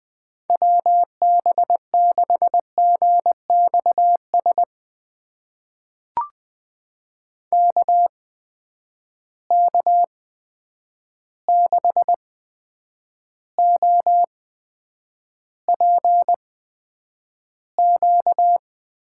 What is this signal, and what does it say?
Morse code, heard as a keyed tone, about 20 wpm, spaced as for 9 wpm, 700 Hz. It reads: WB6GXS KK6OPQ
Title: USA Call Signs - 6 Characters - 8x Character Spacing - 20wpm